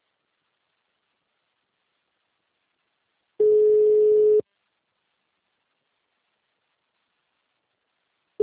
Ringback tone (Europe, 425 Hz).
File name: out- -2006- - - .10606